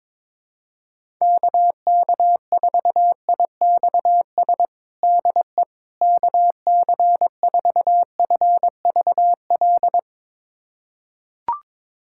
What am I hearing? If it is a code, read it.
Morse code, heard as a keyed tone, about 21 wpm, 700 Hz. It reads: KK4IXS DE KC4FVL
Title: USA Callsign DE Callsign - 6 Characters - 22wpm